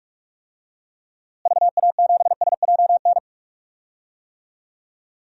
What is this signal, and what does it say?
Morse code, heard as a keyed tone, about 45 wpm, 700 Hz. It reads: VA7SJN
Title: DXCC-100 Call Signs - 6 Characters - Repeated 3x - 45wpm